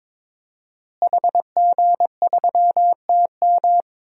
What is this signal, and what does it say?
Morse code, heard as a keyed tone, about 21 wpm, 700 Hz. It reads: HG3TM
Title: DXCC-100 Call Signs - 5 Characters - Repeated 2x - 22wpm